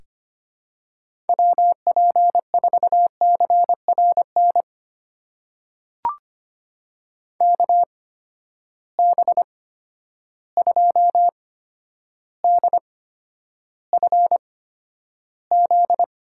Morse code, heard as a keyed tone, about 25 wpm, spaced as for 11 wpm, 700 Hz. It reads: WP4CRN KB2DFZ
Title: USA Call Signs - 6 Characters - 8x Character Spacing - 25wpm